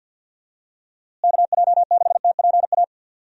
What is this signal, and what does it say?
Morse code, heard as a keyed tone, about 50 wpm, 700 Hz. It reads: KJ6TPA
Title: USA Call Signs - 6 Characters - Repeated 2x - 50wpm